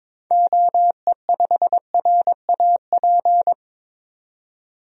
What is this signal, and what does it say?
Morse code, heard as a keyed tone, about 22 wpm, 700 Hz. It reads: OE5RAP
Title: DXCC-100 Call Signs - 6 Characters - 22wpm